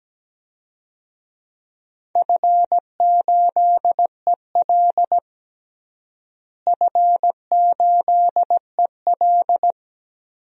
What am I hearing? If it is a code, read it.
Morse code, heard as a keyed tone, about 17 wpm, 700 Hz. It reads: F8EL F8EL
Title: DXCC-100 Call Signs - 4 Characters - Repeated 3x - 17wpm